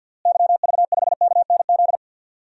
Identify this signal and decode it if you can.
Morse code, heard as a keyed tone, about 51 wpm, 700 Hz. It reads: YV5KNZ
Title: DXCC-100 Call Signs - 6 Characters - Repeated 3x - 50wpm